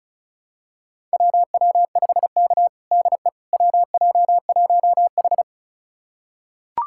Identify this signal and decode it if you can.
Morse code, heard as a keyed tone, about 36 wpm, 700 Hz. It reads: WW5K DE WJ1H
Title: USA Callsign DE Callsign - 4 Characters - 35wpm